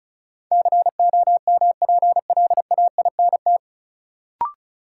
Morse code, heard as a keyed tone, about 34 wpm, 700 Hz. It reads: COMPLAINT
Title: Sentences from Top 1400 Words - New Word Drill Alt - 35wpm